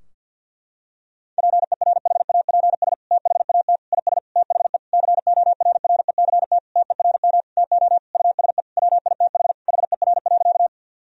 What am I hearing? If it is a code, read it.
Morse code, heard as a keyed tone, about 50 wpm, 700 Hz. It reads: PERHAPS THAT IS THE CORRECT TERM TO USE WITH HER.
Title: Sentences from Top 1000 Words - Review Part 4 - 50wpm